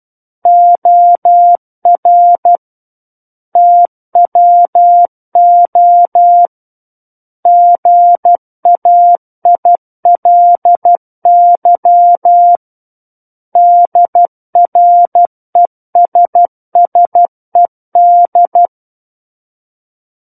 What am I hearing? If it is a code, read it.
Morse code, heard as a keyed tone, about 12 wpm, 700 Hz. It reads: OR TWO GAILY DRESSED
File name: Book One / 03-Chapter 3 - 12 WPM